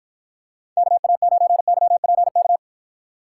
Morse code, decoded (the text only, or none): KA9YPK